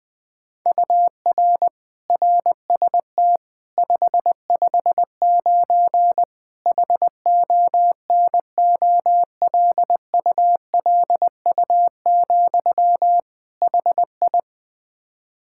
Morse code, heard as a keyed tone, about 21 wpm, 700 Hz. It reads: UR RST 559 HONOLULU, HI
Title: Practice QSO - Exchange 1 - Beginning - 20wpm